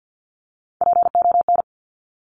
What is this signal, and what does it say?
Morse code, heard as a keyed tone, about 50 wpm, 700 Hz. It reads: PON
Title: Top Three Letter Combinations in Words 50wpm